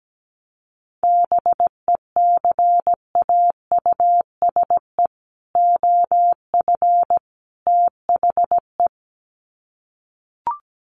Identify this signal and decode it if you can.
Morse code, heard as a keyed tone, about 17 wpm, 700 Hz. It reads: BECAUSE OF THE
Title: Sets of 3 Words 17wpm